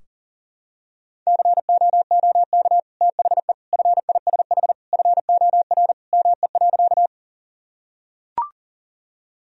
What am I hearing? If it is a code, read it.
Morse code, heard as a keyed tone, about 40 wpm, 700 Hz. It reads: COOK THE FISH FOR ME.
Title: Sentences from Top 1000 Words - Review Part 4 - 40wpm